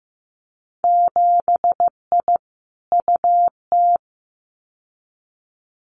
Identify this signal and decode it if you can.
Morse code, heard as a keyed tone, about 15 wpm, 700 Hz. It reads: 7I UT
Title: Winter Field Day Exchanges Encore 15wpm